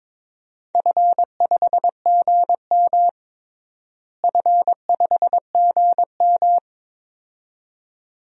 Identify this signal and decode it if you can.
Morse code, heard as a keyed tone, about 22 wpm, 700 Hz. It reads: F5GM F5GM
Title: DXCC-100 Call Signs - 4 Characters - Repeated 2x - 22wpm